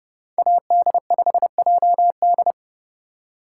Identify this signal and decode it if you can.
Morse code, heard as a keyed tone, about 30 wpm, 700 Hz. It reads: AD5JD
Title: USA Call Signs - 5 Characters - 2x Character Spacing - 30wpm